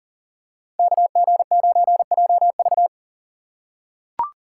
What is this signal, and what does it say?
Morse code, heard as a keyed tone, about 40 wpm, 700 Hz. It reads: KG9JV
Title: USA Call Signs - 3x Character Spacing - 40wpm